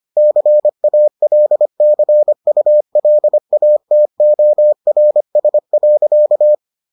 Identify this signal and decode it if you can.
Morse code, heard as a keyed tone, about 25 wpm, 600 Hz. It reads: CALCULATORS.